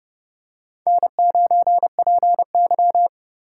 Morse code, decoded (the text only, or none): N9PY